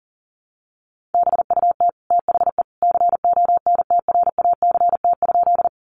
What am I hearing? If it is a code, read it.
Morse code, heard as a keyed tone, about 40 wpm, 700 Hz. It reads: BUT THE CONTRACT?
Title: Sentences from Top 1400 Words 40wpm